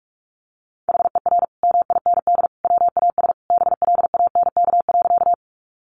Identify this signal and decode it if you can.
Morse code, heard as a keyed tone, about 44 wpm, 700 Hz. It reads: HER MIND WAS BLANK.
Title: Sentences from Top 1500 Words 45wpm